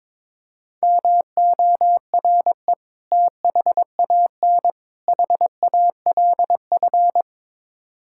Morse code, heard as a keyed tone, about 22 wpm, 700 Hz. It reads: MORE THAN HALF
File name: practice / 22wpm / Build A Phrase - 5-Word Phrases - 22wpm